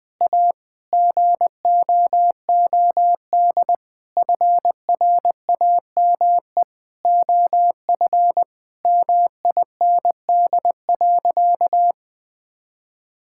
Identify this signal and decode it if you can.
Morse code, heard as a keyed tone, about 20 wpm, 700 Hz. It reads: A GOOD FRAME OF MIND.
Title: Sentences from Top 1000 Words - Review Part 6 - 20wpm